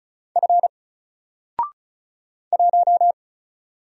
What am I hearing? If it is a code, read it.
Morse code, heard as a keyed tone, about 36 wpm, 700 Hz. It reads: F 1
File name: Single-Letter-Number-35wpm